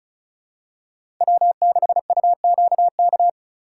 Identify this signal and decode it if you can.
Morse code, heard as a keyed tone, about 34 wpm, 700 Hz. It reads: W6UQK
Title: USA Call Signs - 5 Characters - 2x Character Spacing - 35wpm